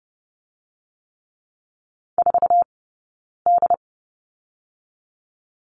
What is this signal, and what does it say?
Morse code, heard as a keyed tone, about 31 wpm, 700 Hz. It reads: 4 D
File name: Lesson 19d - Letter w - Two Characters 30wpm